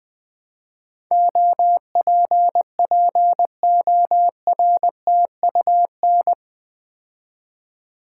Morse code, heard as a keyed tone, about 20 wpm, 700 Hz. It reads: OPPORTUN
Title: Build A Word - 4 Letters At A Time - 11-Letter Words - 20wpm